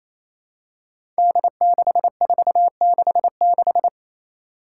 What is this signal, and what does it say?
Morse code, heard as a keyed tone, about 27 wpm, 700 Hz. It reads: D6466
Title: Five Challenging Letter-Number Combinations 28wpm